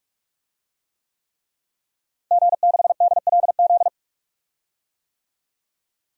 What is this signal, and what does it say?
Morse code, heard as a keyed tone, about 46 wpm, 700 Hz. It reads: G6DLZ
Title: DXCC-100 Call Signs - 5 Characters - Repeated 3x - 45wpm